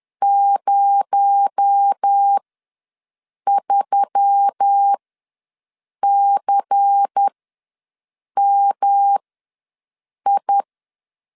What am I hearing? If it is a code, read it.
Morse code, heard as a keyed tone, about 10 wpm, spaced as for 6 wpm, 800 Hz. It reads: 03CMI
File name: LEZIONI / 6 PKX0 / 6 PKX0 ws 10wpm 4220